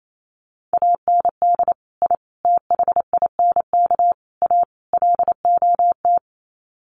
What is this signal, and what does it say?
Morse code, heard as a keyed tone, about 28 wpm, 700 Hz. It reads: AND I THINK A LOT
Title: Sets of 5 Words Encore2 28wpm